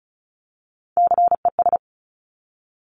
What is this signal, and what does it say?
Morse code, heard as a keyed tone, about 36 wpm, 690 Hz. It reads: CES
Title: Top Three Letter Combinations in Words 35wpm